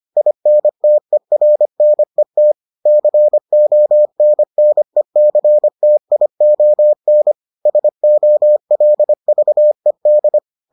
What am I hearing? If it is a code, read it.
Morse code, heard as a keyed tone, about 25 wpm, 600 Hz. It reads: INTERNET CONNECTION SOLVED